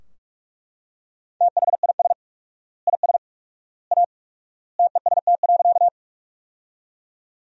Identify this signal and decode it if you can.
Morse code, heard as a keyed tone, about 41 wpm, 700 Hz. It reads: THIS IS A TEST.
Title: 4x Word-Spacing - Sentences from Top 500 Words - 45wpm